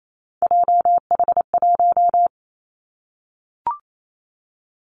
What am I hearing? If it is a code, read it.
Morse code, heard as a keyed tone, about 28 wpm, 700 Hz. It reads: JH1
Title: Callsign Prefixes for Top 25 DXCC - 28wpm